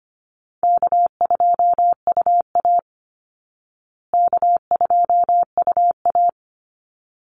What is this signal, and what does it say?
Morse code, heard as a keyed tone, about 24 wpm, 700 Hz. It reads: K2UA K2UA
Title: Easy Call Signs Repeated 2x - 25wpm